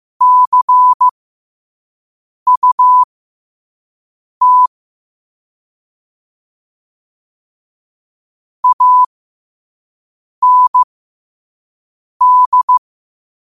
Morse code, heard as a keyed tone, about 15 wpm, spaced as for 5 wpm, 1000 Hz. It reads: CUT AND
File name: qotdfiles / 5wpm